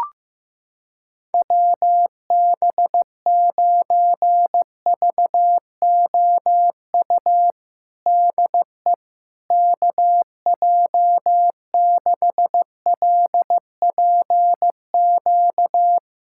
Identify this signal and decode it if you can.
Morse code, heard as a keyed tone, about 15 wpm, 700 Hz. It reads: WB9VOU DE KJ6LPQ
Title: USA Callsign DE Callsign - 6 Characters - 15wpm